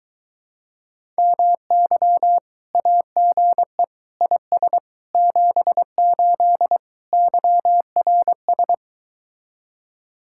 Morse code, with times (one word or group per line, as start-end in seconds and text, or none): 1.2-2.4 MY
2.7-3.8 AGE
4.2-4.8 IS
5.2-6.8 78
7.1-8.7 YRS